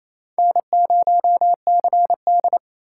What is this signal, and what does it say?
Morse code, heard as a keyed tone, about 28 wpm, 700 Hz. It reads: N0CD
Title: USA Call Signs - 4 Characters - 4x Character Spacing - 28wpm